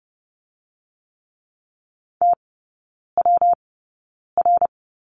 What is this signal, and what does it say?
Morse code, heard as a keyed tone, about 30 wpm, spaced as for 9 wpm, 700 Hz. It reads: TWR
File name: Lesson 19e - Letter w - Three Characters 30wpm